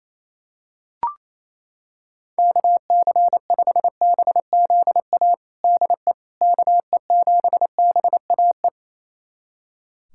Morse code, heard as a keyed tone, about 28 wpm, 700 Hz. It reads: KC5BZA DE KE7BAE
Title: USA Callsign DE Callsign 28wpm